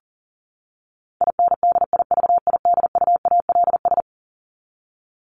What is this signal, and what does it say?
Morse code, heard as a keyed tone, about 40 wpm, 700 Hz. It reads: INDIVIDUALS
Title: ES:S Suffix - Top 100 Words with Base Words 40wpm